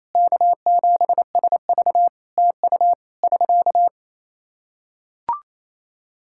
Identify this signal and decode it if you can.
Morse code, heard as a keyed tone, about 28 wpm, 700 Hz. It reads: K7SV TU <SK>
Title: Practice QSO - Exchange 4 - Ending - 28wpm